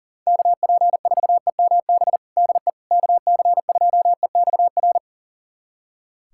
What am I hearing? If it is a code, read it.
Morse code, heard as a keyed tone, about 40 wpm, 700 Hz. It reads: KP4EMB DE KC2EXR
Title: USA Callsign DE Callsign 40wpm